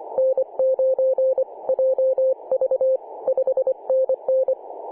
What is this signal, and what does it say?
Morse code, heard as a keyed tone, about 25 wpm, 520 Hz. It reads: N9JV 5NN
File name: 2013/Sep/22 2149z A35JP/H 21.010MHz CW